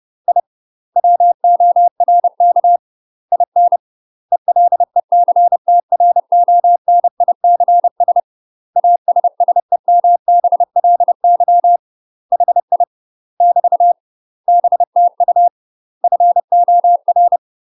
Morse code, heard as a keyed tone, about 30 wpm, 700 Hz. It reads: I WORK IN ELECTRONICS ASSEMBLY HI <BT> BTU FOR
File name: UK MASTER QSOs TRAINING / 30WPM / UK MASTER QSOs 0023